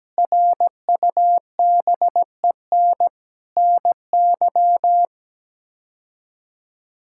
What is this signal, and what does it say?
Morse code, heard as a keyed tone, about 17 wpm, 700 Hz. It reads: RUBEN NY
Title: K1USN SST QSO Practice - 17wpm